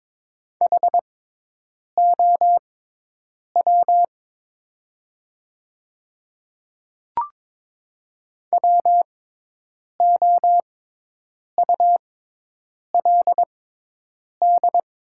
Morse code, heard as a keyed tone, about 22 wpm, spaced as for 8 wpm, 700 Hz. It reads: HOW WOULD
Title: 6x Character Spacing - Top 100 Words - 22wpm